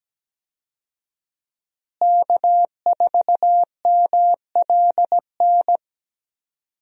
Morse code, heard as a keyed tone, about 17 wpm, 700 Hz. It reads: K4MLN
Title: USA Call Signs - 5 Characters - Repeated 3x - 17wpm